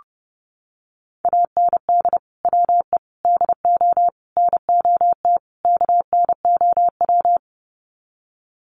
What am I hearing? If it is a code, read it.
Morse code, heard as a keyed tone, about 30 wpm, 700 Hz. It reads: AND WE DO NOT KNOW
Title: Sets of 5 Words Encore 30wpm